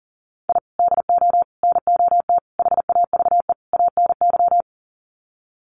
Morse code, heard as a keyed tone, about 40 wpm, 700 Hz. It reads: I DO NOT HAVE ANY
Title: Sets of 5 Words 40wpm